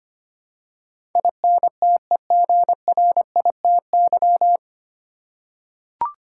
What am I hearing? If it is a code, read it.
Morse code, heard as a keyed tone, about 25 wpm, 700 Hz. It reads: INTEGRITY
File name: practice / 25wpm / 9 Letter Words - First 3 Letters Given - 25wpm